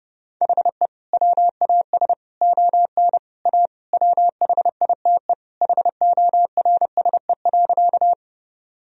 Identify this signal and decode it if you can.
Morse code, heard as a keyed tone, about 29 wpm, 700 Hz. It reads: HE WAS ON A WHITE HORSE.
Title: Sentences from Top 1000 Words - Review Part 1 - 30wpm